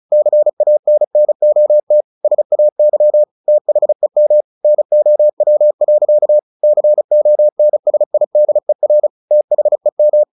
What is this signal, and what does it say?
Morse code, heard as a keyed tone, about 34 wpm, 600 Hz. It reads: CANNOT SAY THEM NOW. CONSIDER THEM
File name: the old man and the sea 2 35wpm